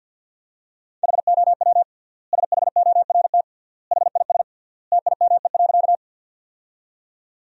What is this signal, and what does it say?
Morse code, heard as a keyed tone, about 50 wpm, 700 Hz. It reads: HOW SHORT HIS TIME.
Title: 3x Word-Spacing - Sentences from Top 400 Words - 50wpm